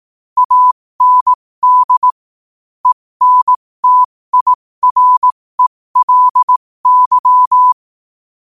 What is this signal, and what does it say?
Morse code, heard as a keyed tone, about 18 wpm, spaced as for 15 wpm, 1000 Hz. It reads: AND ENTIRELY
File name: qotdfiles / 15wpm